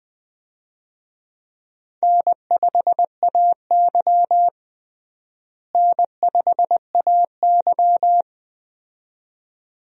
Morse code, heard as a keyed tone, about 20 wpm, 700 Hz. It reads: N5AY N5AY
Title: USA Call Signs - 4 Characters - Repeated 2x - 20wpm